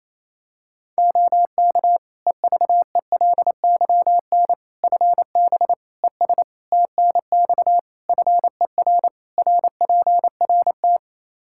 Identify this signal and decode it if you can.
Morse code, heard as a keyed tone, about 28 wpm, 700 Hz. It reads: OK EVELYN FB ES TNX FER RPRT
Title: Practice QSO - Exchange 2 - Info - 28wpm